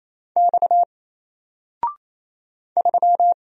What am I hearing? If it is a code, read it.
Morse code, heard as a keyed tone, about 27 wpm, 700 Hz. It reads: X 3
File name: Single-Letter-Number-28wpm